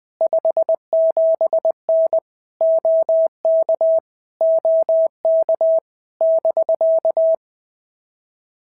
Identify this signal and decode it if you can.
Morse code, heard as a keyed tone, about 20 wpm, 650 Hz. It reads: 57N OK OK <BK>
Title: POTA QSO Practice - 20wpm